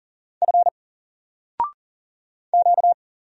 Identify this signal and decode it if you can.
Morse code, heard as a keyed tone, about 40 wpm, 700 Hz. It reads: F Q
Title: Single Letter-Number 40wpm